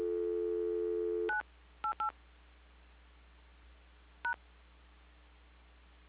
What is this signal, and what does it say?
DTMF, heard as telephone touch tones, tones of about 110 ms. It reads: 988 #
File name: recordings / PortlandUnion Millennium